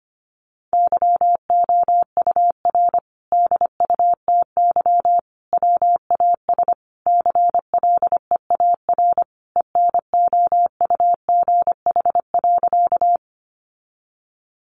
Morse code, heard as a keyed tone, about 25 wpm, 700 Hz. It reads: YOUR DUTY WAS CLEAR ENOUGH.
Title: Sentences from Top 1500 Words 25wpm